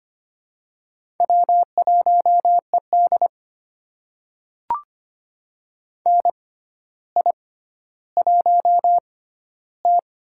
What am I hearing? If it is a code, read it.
Morse code, heard as a keyed tone, about 25 wpm, spaced as for 13 wpm, 700 Hz. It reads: W1ED NI1T
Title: USA Call Signs - 4 Characters - 6x Character Spacing - 25wpm